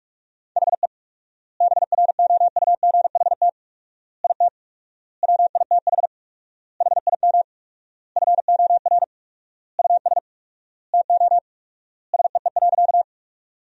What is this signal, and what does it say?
Morse code, heard as a keyed tone, about 46 wpm, 700 Hz. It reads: HE BROUGHT IT WITH HIM FOR US TO SEE.
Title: 4x Word-Spacing - Sentences from Top 500 Words - 45wpm